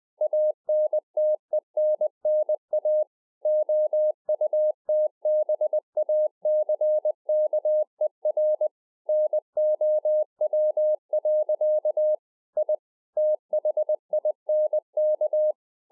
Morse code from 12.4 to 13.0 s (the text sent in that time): I